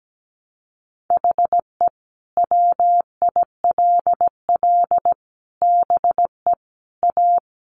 Morse code, heard as a keyed tone, about 17 wpm, 700 Hz. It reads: HE WILL BE A
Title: Sentences from Top 1200 Words 17wpm